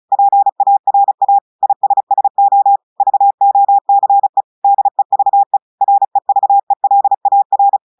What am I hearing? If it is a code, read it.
Morse code, heard as a keyed tone, about 34 wpm, 800 Hz. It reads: PARA ISSO VOCE DEVE REVELAR
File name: portugais texte en clair 2 vitesse 35 mots minutes